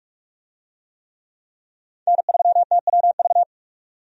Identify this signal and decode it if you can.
Morse code, heard as a keyed tone, about 46 wpm, 700 Hz. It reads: N3TWV
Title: USA Call Signs - 5 Characters - 45wpm